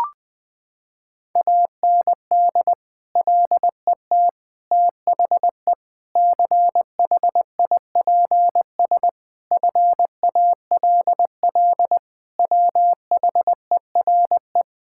Morse code, heard as a keyed tone, about 20 wpm, 700 Hz. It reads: AND LET THE CHIPS FALL WHERE
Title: Sets of 8 Words - 20wpm